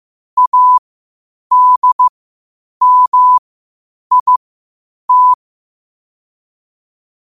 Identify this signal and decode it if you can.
Morse code, heard as a keyed tone, about 15 wpm, spaced as for 8 wpm, 1000 Hz. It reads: ADMIT
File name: qotdfiles / 8wpm